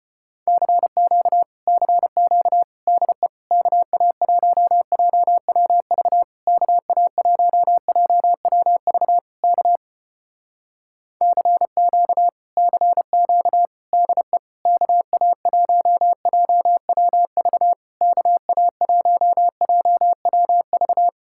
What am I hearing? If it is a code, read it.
Morse code, heard as a keyed tone, about 34 wpm, 700 Hz. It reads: CQ CQ DE KA1JWV KA1JWV K CQ CQ DE KA1JWV KA1JWV